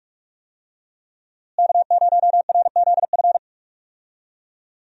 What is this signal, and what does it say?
Morse code, heard as a keyed tone, about 46 wpm, 700 Hz. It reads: K0RZF
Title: USA Call Signs - 5 Characters - Repeated 2x - 45wpm